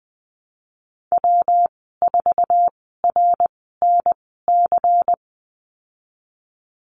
Morse code, heard as a keyed tone, about 20 wpm, spaced as for 15 wpm, 700 Hz. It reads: W4RNC
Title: 122 Lesson 20g - Number 3 - Call Signs 20wpm